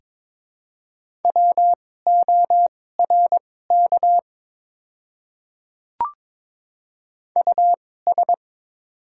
Morse code, heard as a keyed tone, about 22 wpm, spaced as for 16 wpm, 700 Hz. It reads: WORK US
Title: 2x Character Spacing - Top 200 Words - 22wpm